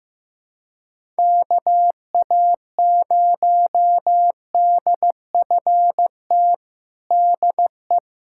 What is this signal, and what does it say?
Morse code, heard as a keyed tone, about 15 wpm, 700 Hz. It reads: KA0DFT DE
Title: USA Callsign DE Callsign - 6 Characters - 15wpm